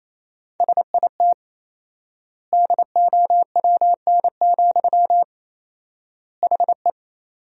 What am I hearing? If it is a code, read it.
Morse code, heard as a keyed tone, about 28 wpm, 700 Hz. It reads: SIT DOWN, HE
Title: 4x Word-Spacing - Sentences from Top 500 Words - 28wpm